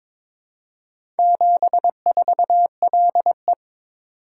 Morse code, heard as a keyed tone, about 22 wpm, 700 Hz. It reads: 74LE
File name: practice / 22wpm / Four Letter-Number Combinations - 22wpm